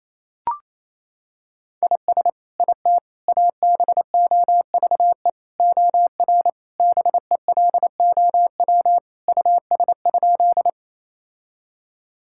Morse code, heard as a keyed tone, about 28 wpm, 700 Hz. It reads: IS IT ABOVE OR BELOW US?
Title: Sentences from Top 1000 Words - Review Part 5 - 28wpm